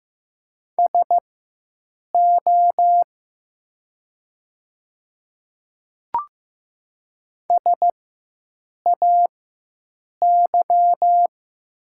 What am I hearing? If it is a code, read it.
Morse code, heard as a keyed tone, about 15 wpm, spaced as for 7 wpm, 700 Hz. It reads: SO SAY
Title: 4x Character Spacing - Top 300 Words - 15wpm